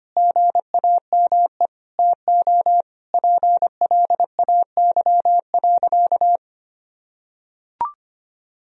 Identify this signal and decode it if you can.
Morse code, heard as a keyed tone, about 25 wpm, 700 Hz. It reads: GAME TO PLAY.
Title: Sentences from Top 1000 Words - Review Part 3 - 25wpm